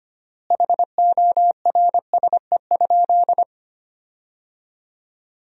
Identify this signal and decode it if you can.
Morse code, heard as a keyed tone, about 25 wpm, 700 Hz. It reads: HORSE?
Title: 3x Word-Spacing - Sentences from Top 500 Words - 25wpm